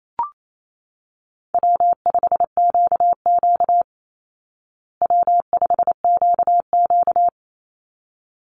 Morse code, heard as a keyed tone, about 28 wpm, 700 Hz. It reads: W5QQ W5QQ
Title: Call Signs Repeated 3x - 28wpm